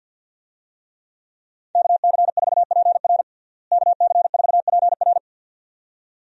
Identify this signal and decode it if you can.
Morse code, heard as a keyed tone, about 51 wpm, 700 Hz. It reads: KC4PR KC4PR
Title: USA Call Signs - 5 Characters - Repeated 2x - 50wpm